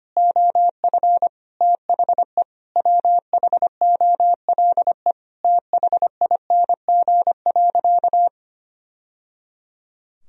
Morse code, heard as a keyed tone, about 25 wpm, 700 Hz. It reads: OF THE WHOLE THING.
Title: Sentences from Top 1000 Words - Review Part 1 - 25wpm